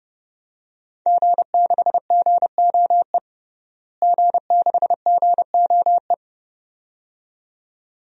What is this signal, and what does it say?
Morse code, heard as a keyed tone, about 30 wpm, 700 Hz. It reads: G6GOE G6GOE
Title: DXCC-100 Call Signs - 5 Characters - Repeated 2x - 30wpm